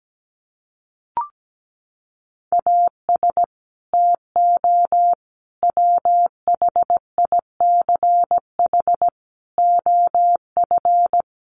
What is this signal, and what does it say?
Morse code, heard as a keyed tone, about 17 wpm, 700 Hz. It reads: AS TO WHICH OF
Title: Sentences from Top 1200 Words 17wpm